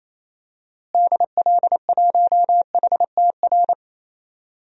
Morse code, heard as a keyed tone, about 28 wpm, 700 Hz. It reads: DL1HTR
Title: DXCC-100 Call Signs - 6 Characters - Repeated 2x - 28wpm